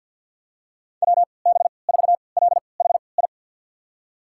Morse code, heard as a keyed tone, about 48 wpm, spaced as for 28 wpm, 700 Hz. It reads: WB4LHI
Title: USA Call Signs - 6 Characters - 3x Character Spacing - 50wpm